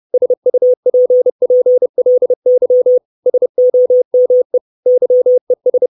Morse code, heard as a keyed tone, about 30 wpm, 500 Hz. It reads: SUPPLY SOME YES
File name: 632 Words 30WPM